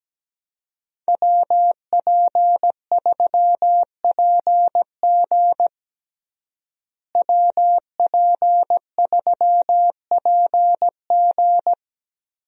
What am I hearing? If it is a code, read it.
Morse code, heard as a keyed tone, about 17 wpm, 700 Hz. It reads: WP3PG WP3PG
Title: USA Call Signs - 5 Characters - Repeated 3x - 17wpm